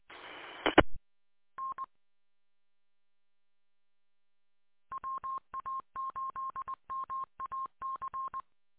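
Morse code, heard as a keyed tone, about 20 wpm, 1070 Hz. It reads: I N WA8MAC
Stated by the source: Morse code recorded on a 147.200 repeater audio archive.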